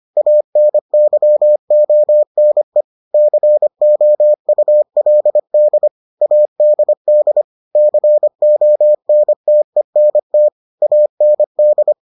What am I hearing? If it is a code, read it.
Morse code, heard as a keyed tone, about 25 wpm, 600 Hz. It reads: ANYONE COULD ADD CONTENT AND